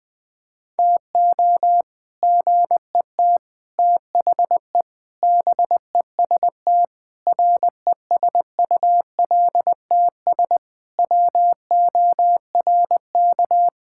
Morse code, heard as a keyed tone, about 20 wpm, 700 Hz. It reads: TO GET THE BEST RESULTS WORK
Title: Sentences with 3-Word Phrases - 20wpm